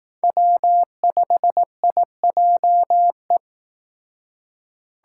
Morse code, heard as a keyed tone, about 18 wpm, 700 Hz. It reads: W5IJE
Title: Speed Racing - USA 1x3 Call Signs 15wpm